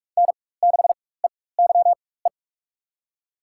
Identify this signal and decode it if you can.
Morse code, heard as a keyed tone, about 44 wpm, spaced as for 21 wpm, 700 Hz. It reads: N6EYE
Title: USA Call Signs - 4x Character Spacing - 45wpm